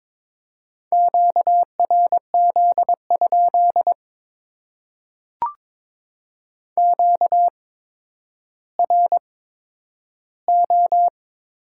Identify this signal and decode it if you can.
Morse code, heard as a keyed tone, about 22 wpm, spaced as for 11 wpm, 700 Hz. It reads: QRZ? QRO
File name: Top 90 QSO Elements - 8x Character Spacing - 22wpm